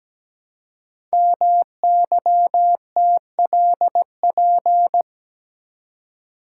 Morse code, heard as a keyed tone, about 17 wpm, 700 Hz. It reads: MYTLP
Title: Code Groups - 5 Letters - 17wpm